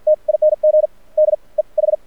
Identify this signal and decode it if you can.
Morse code, heard as a keyed tone, about 50 wpm, spaced as for 36 wpm, 610 Hz. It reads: TING DEH
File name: archived / CRSnet